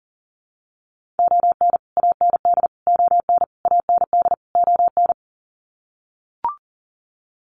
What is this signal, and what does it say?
Morse code, heard as a keyed tone, about 40 wpm, 700 Hz. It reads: ON AND ON AND ON
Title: Sets of 5 Words Encore2 40wpm